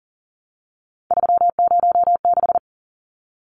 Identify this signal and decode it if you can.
Morse code, heard as a keyed tone, about 40 wpm, 700 Hz. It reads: 306
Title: Numbers - 3-Digit 40wpm